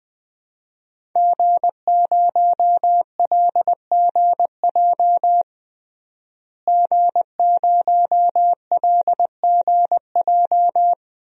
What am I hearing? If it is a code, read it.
Morse code, heard as a keyed tone, about 20 wpm, 700 Hz. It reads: G0LGJ G0LGJ
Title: DXCC-100 Call Signs - 5 Characters - Repeated 2x - 20wpm